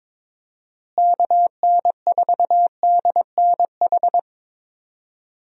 Morse code, heard as a keyed tone, about 22 wpm, 700 Hz. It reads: KN4DNH
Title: USA Call Signs - 6 Characters - Repeated 3x - 22wpm